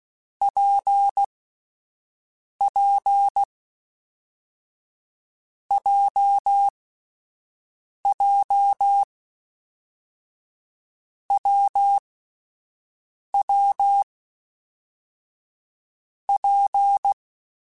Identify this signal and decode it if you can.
Morse code, heard as a keyed tone, about 16 wpm, 800 Hz. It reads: P P J J W W P
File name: Difficult Combos / Difficult Combos-mp3s / WPJ WPJ1 @10